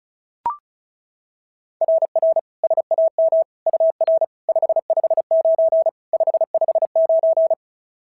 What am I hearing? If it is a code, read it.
Morse code, heard as a keyed tone, about 36 wpm, 690 Hz. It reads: RR SAM UR 559 559
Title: SOTA QSO Practice 35wpm